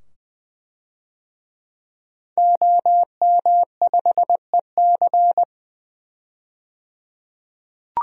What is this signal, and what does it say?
Morse code, heard as a keyed tone, about 20 wpm, 700 Hz. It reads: OM5EC
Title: DXCC-100 Call Signs - 5 Characters - 20wpm